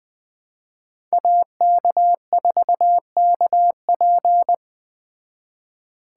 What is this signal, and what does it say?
Morse code, heard as a keyed tone, about 20 wpm, 700 Hz. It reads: AK4KP